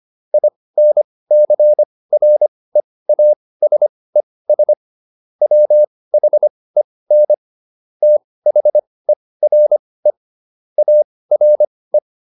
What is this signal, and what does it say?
Morse code, heard as a keyed tone, about 25 wpm, spaced as for 18 wpm, 600 Hz. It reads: INCREASES WHEN THERE ARE
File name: SS211 18